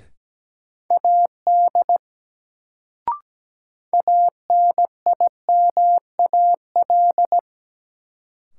Morse code, heard as a keyed tone, about 17 wpm, 700 Hz. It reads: AD ANIMAL
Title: Sentences from Top 1300 Words - New Word Drill Alt - 17wpm